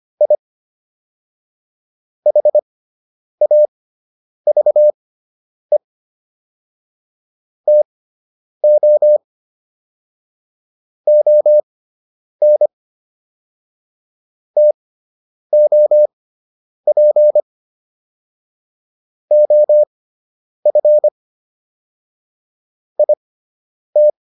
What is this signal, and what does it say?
Morse code, heard as a keyed tone, about 25 wpm, spaced as for 9 wpm, 600 Hz. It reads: I HAVE TO ON TOP OF IT